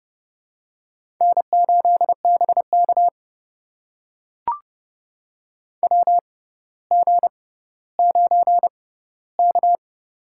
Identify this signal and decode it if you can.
Morse code, heard as a keyed tone, about 30 wpm, spaced as for 15 wpm, 700 Hz. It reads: N8BK WG9K
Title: USA Call Signs - 4 Characters - 6x Character Spacing - 30wpm